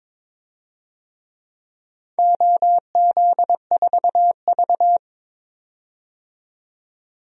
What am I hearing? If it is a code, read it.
Morse code, heard as a keyed tone, about 22 wpm, 700 Hz. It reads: OZ4V
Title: DXCC-100 Call Signs - 4 Characters - 22wpm